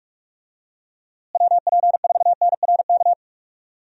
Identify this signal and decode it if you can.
Morse code, heard as a keyed tone, about 46 wpm, 700 Hz. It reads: WP4NRK
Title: USA Call Signs - 6 Characters - Repeated 3x - 45wpm